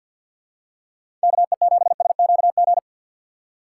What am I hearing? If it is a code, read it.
Morse code, heard as a keyed tone, about 51 wpm, 700 Hz. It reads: KE7SQG
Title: USA Call Signs - 6 Characters - 4x Character Spacing - 50wpm